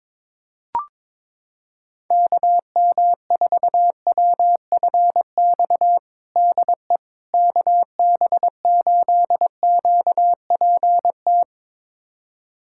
Morse code, heard as a keyed tone, about 22 wpm, 700 Hz. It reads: KM4WFX DE KB8QPT
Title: USA Callsign DE Callsign - 6 Characters - 22wpm